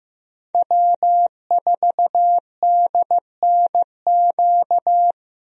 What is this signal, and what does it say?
Morse code, heard as a keyed tone, about 16 wpm, 700 Hz. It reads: W4DNQ
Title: USA Call Signs - 5 Characters - Repeated 3x - 15wpm